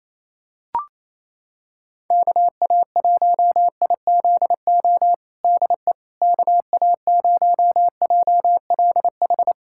Morse code, heard as a keyed tone, about 27 wpm, 700 Hz. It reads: KA1IZO DE KA0JLH
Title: USA Callsign DE Callsign - 6 Characters - 28wpm